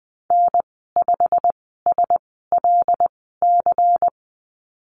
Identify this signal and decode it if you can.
Morse code, heard as a keyed tone, about 20 wpm, spaced as for 14 wpm, 700 Hz. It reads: N5SLC
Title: 167 Lesson 27g - Number 7 - Call Signs 20wpm